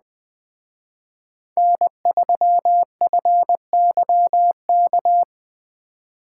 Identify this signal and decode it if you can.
Morse code, heard as a keyed tone, about 20 wpm, 700 Hz. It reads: N3FYK
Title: Speed Racing - USA 1x3 Call Signs 17wpm